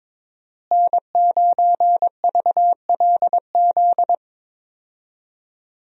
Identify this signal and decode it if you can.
Morse code, heard as a keyed tone, about 22 wpm, 700 Hz. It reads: N9VLZ
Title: USA Call Signs - 5 Characters - Repeated 2x - 22wpm